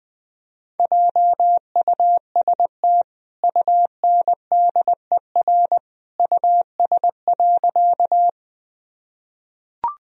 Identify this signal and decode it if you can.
Morse code, heard as a keyed tone, about 20 wpm, 700 Hz. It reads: JUST UNDER US.
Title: Sentences from Top 1000 Words - Review Part 1 - 20wpm